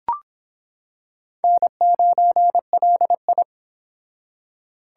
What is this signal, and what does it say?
Morse code, heard as a keyed tone, about 26 wpm, 700 Hz. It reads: N9LI